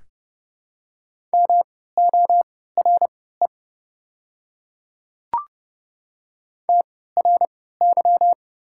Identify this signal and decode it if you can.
Morse code, heard as a keyed tone, about 30 wpm, spaced as for 17 wpm, 690 Hz. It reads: MORE TRY
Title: 3x Character Spacing - Top 200 Words - 30wpm